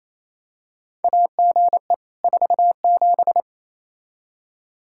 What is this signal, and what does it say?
Morse code, heard as a keyed tone, about 27 wpm, 700 Hz. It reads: AGE 47
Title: Practice QSO - Exchange 3 - More Info - 28wpm